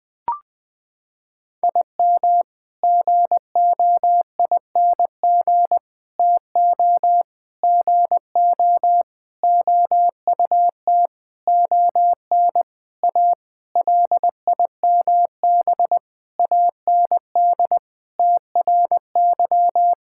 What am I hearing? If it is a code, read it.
Morse code, heard as a keyed tone, about 20 wpm, 700 Hz. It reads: IM GOING TO GO OUT ON A LIMB AND TRY
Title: Sentences with 8-Word Phrases - 20wpm